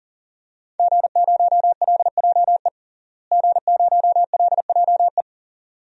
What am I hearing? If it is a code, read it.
Morse code, heard as a keyed tone, about 40 wpm, 700 Hz. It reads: G0LJE G0LJE
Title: DXCC-100 Call Signs - 5 Characters - Repeated 2x - 40wpm